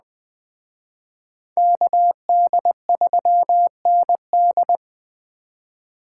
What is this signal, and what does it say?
Morse code, heard as a keyed tone, about 20 wpm, 700 Hz. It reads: KD3ND
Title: Speed Racing - USA 2x2 Call Signs 17wpm